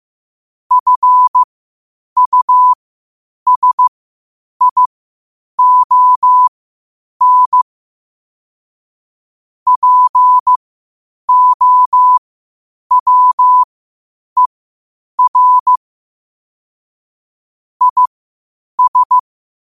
Morse code, heard as a keyed tone, about 15 wpm, spaced as for 8 wpm, 1000 Hz. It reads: FUSION POWER IS
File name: qotdfiles / 8wpm